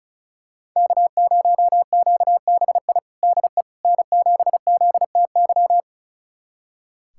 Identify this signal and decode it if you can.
Morse code, heard as a keyed tone, about 34 wpm, 700 Hz. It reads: K0QBI DE N7ZTY
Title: USA Callsign DE Callsign - 5 Characters - 35wpm